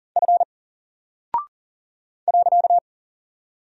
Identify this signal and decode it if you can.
Morse code, heard as a keyed tone, about 40 wpm, 700 Hz. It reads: F .
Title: Single Character (Letter:Number:Punctuation:Prosign) 40wpm